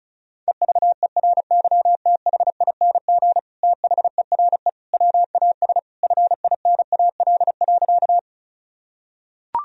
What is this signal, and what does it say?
Morse code, heard as a keyed tone, about 34 wpm, 700 Hz. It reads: EVERYTHING THERE WAS FINAL.
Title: Sentences from Top 1000 Words - Review Part 2 - 35wpm